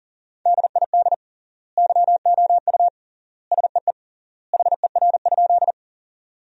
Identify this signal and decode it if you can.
Morse code, heard as a keyed tone, about 40 wpm, 700 Hz. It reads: DID YOU SEE HER?
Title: 3x Word-Spacing - Sentences from Top 100 Words - 40wpm